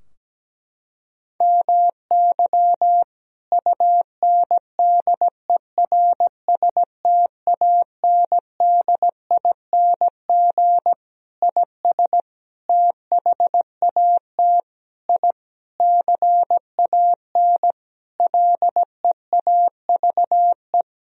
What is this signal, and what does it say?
Morse code, heard as a keyed tone, about 17 wpm, 700 Hz. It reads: MY UNDERSTANDING IS THAT I CAN LEAVE
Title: Sentences from Top 1000 Words - Review Part 3 - 17wpm